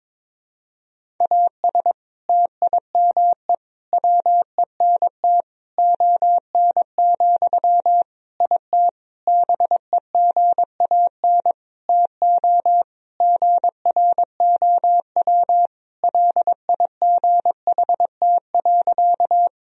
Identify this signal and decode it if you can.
Morse code, heard as a keyed tone, about 22 wpm, 700 Hz. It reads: AS TIME WENT ON, IT BEGAN TO GROW LIGHT.
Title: Sentences from Top 1000 Words - Review Part 1 - 22wpm